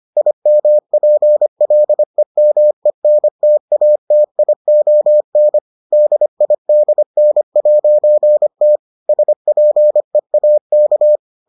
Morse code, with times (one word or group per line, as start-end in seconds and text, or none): 0.2-5.6 IMPLEMENTATION
5.9-8.8 DIDN'T
9.1-11.2 SPEAK